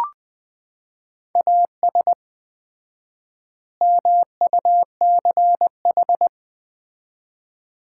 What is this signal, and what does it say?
Morse code, heard as a keyed tone, about 20 wpm, 700 Hz. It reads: AS MUCH
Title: 4x Word-Spacing - Sets of 3 Words - 20wpm